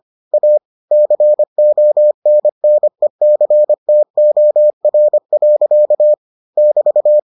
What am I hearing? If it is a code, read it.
Morse code, heard as a keyed tone, about 25 wpm, 600 Hz. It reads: A CONNECTOR. <BT>